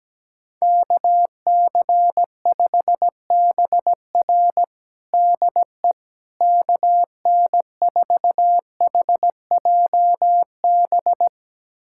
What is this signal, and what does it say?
Morse code, heard as a keyed tone, about 17 wpm, 700 Hz. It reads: KC5BR DE KN4HJB
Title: USA Callsign DE Callsign 17wpm